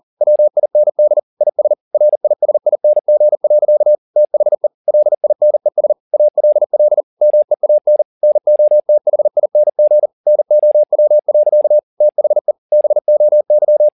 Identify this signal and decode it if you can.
Morse code, heard as a keyed tone, about 40 wpm, 600 Hz. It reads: WIND IS RISING. THE LINES ALL MEAN NOTHING NOW. THE BOY
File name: the old man and the sea 2 40wpm